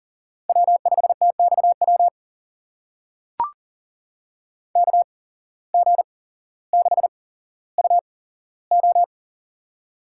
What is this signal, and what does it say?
Morse code, heard as a keyed tone, about 40 wpm, spaced as for 17 wpm, 700 Hz. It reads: W5TXW KG6UO
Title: USA Call Signs - 5 Characters - 8x Character Spacing - 40wpm